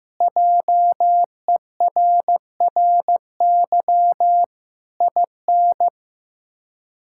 Morse code, heard as a keyed tone, about 15 wpm, 700 Hz. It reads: JERRY IN
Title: K1USN SST QSO Practice - 15wpm